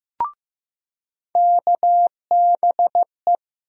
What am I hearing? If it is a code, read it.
Morse code, heard as a keyed tone, about 15 wpm, 700 Hz. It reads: KBE
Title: USA Call Sign Postfixes - 3 Letters 15wpm